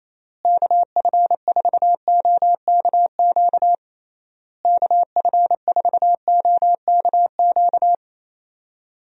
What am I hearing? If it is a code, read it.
Morse code, heard as a keyed tone, about 27 wpm, 700 Hz. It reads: KF4OKQ KF4OKQ
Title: USA Call Signs - 6 Characters - Repeated 3x - 28wpm